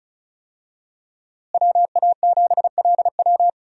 Morse code, heard as a keyed tone, about 36 wpm, 700 Hz. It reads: WA7LW
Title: USA Call Signs - 5 Characters - Repeated 3x - 35wpm